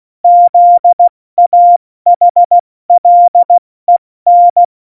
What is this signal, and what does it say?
Morse code, heard as a keyed tone, about 16 wpm, spaced as for 14 wpm, 700 Hz. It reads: ZAHLEN
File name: Morsekurs / Witze / Witz 15. Oktober 2019, 70 F80